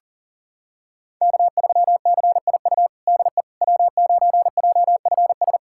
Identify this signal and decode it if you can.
Morse code, heard as a keyed tone, about 40 wpm, 700 Hz. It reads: K3CIU DE W9JFS
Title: USA Callsign DE Callsign - 5 Characters - 40wpm